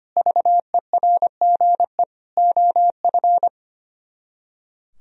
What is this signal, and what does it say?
Morse code, heard as a keyed tone, about 32 wpm, 700 Hz. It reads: VERGE OF